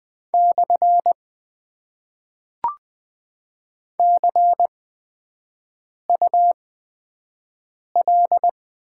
Morse code, heard as a keyed tone, about 20 wpm, spaced as for 5 wpm, 700 Hz. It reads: / CUL
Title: Top 90 QSO Elements - 8x Character Spacing - 20wpm